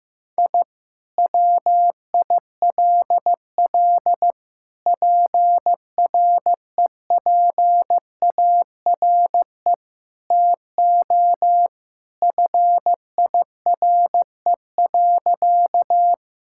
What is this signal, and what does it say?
Morse code, heard as a keyed tone, about 15 wpm, 700 Hz. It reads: I WILL PREPARE TO FIRE.
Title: Sentences from Top 1000 Words - Review Part 6 - 15wpm